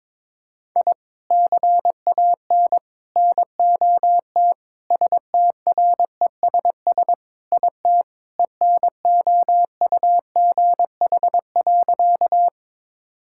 Morse code, heard as a keyed tone, about 21 wpm, 700 Hz. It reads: I CAN NOT STRESS IT ENOUGH.
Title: Sentences from Top 1000 Words - Review Part 6 - 22wpm